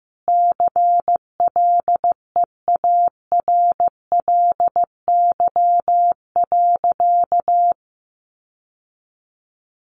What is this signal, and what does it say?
Morse code, heard as a keyed tone, about 15 wpm, 700 Hz. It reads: CLEARLY.
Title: Sentences from Top 1300 Words 15wpm